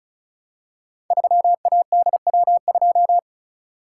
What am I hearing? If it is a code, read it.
Morse code, heard as a keyed tone, about 36 wpm, 690 Hz. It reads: 3ADW2
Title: Code Groups - 5 Letter:Numbers - 35wpm